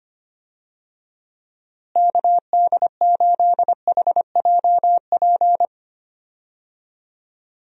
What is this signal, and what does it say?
Morse code, heard as a keyed tone, about 25 wpm, 700 Hz. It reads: KD8HJP
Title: USA Call Signs - 6 Characters - Repeated 2x - 25wpm